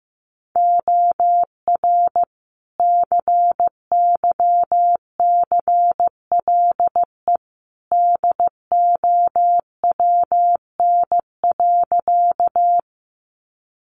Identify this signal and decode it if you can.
Morse code, heard as a keyed tone, about 15 wpm, 700 Hz. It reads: OR CYCLE DOWN.
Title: Sentences from Top 900 Words 15wpm